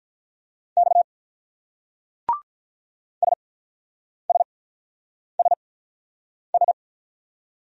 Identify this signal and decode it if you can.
Morse code, heard as a keyed tone, about 42 wpm, spaced as for 8 wpm, 690 Hz. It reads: X SSSS
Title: Speed-Racing Single Letters 35wpm